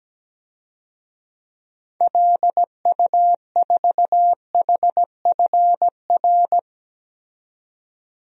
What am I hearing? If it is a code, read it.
Morse code, heard as a keyed tone, about 17 wpm, 700 Hz. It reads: LU4HFR
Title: DXCC-100 Call Signs - 6 Characters - 17wpm